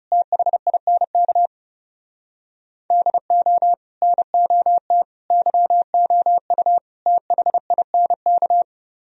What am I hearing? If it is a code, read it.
Morse code, heard as a keyed tone, about 39 wpm, 700 Hz. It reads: THINK DO NOT YOU THINK